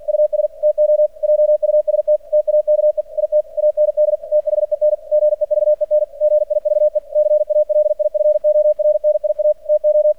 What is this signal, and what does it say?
Morse code, heard as a keyed tone, about 45 wpm, 600 Hz. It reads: US TO JUST TAKE IT AND THEN GEVEN GIVE CALIFORNIA TO
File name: archived / CRSnet